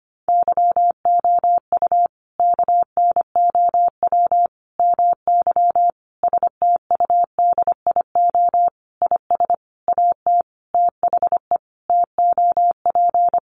Morse code, heard as a keyed tone, about 25 wpm, 700 Hz. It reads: YOU KNOW MY STUDIO IS AT THE TOP
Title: Sentences from Top 1200 Words 25wpm